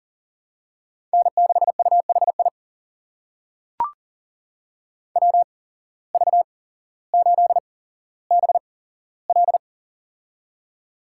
Morse code, heard as a keyed tone, about 40 wpm, spaced as for 17 wpm, 700 Hz. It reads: N6UHI WV8BL
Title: USA Call Signs - 5 Characters - 8x Character Spacing - 40wpm